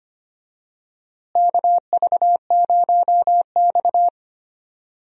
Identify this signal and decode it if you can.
Morse code, heard as a keyed tone, about 25 wpm, 700 Hz. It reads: KV0X
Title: USA Call Signs - 4 Characters - Repeated 2x - 25wpm